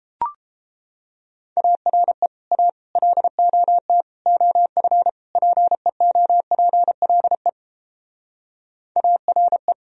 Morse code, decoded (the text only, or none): ARE A LOT OF PEOPLE ARE